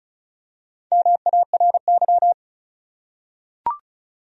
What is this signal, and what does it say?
Morse code, heard as a keyed tone, about 36 wpm, 700 Hz. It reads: MARY
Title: USA Common Names - Max 5 Letters - 35wpm